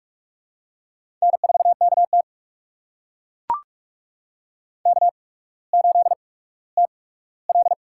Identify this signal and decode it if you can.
Morse code, heard as a keyed tone, about 44 wpm, spaced as for 19 wpm, 700 Hz. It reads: N4KT K8TL
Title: USA Call Signs - 4 Characters - 8x Character Spacing - 45wpm